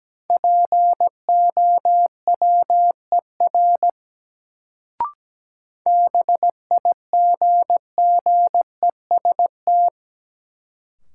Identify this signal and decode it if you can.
Morse code, heard as a keyed tone, about 17 wpm, 700 Hz. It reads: POWER BIGGEST
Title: Sets of 4 Words - New Word Drill - 17wpm